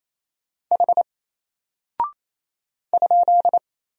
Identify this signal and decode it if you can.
Morse code, heard as a keyed tone, about 28 wpm, 700 Hz. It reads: H ?
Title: Single Character (Letter:Number:Punctuation:Prosign) Alt 28wpm